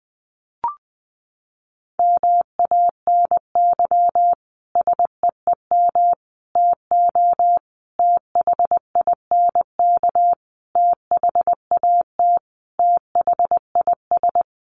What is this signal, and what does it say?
Morse code, heard as a keyed tone, about 20 wpm, 700 Hz. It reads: MANY SEEM TO THINK THAT THIS
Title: Sentences from Top 1400 Words 20wpm